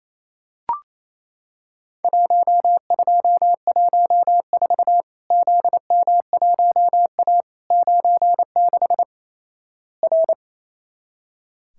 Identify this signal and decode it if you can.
Morse code, heard as a keyed tone, about 28 wpm, 700 Hz. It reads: 1214 ZM1A 96 R
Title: Worked All Europe DX Contest - QTC Exchange - 28wpm